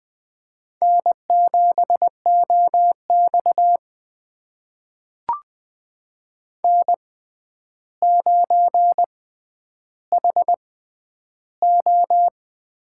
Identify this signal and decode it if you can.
Morse code, heard as a keyed tone, about 20 wpm, spaced as for 10 wpm, 700 Hz. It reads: N7OX N9HO
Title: USA Call Signs - 6x Character Spacing - 20wpm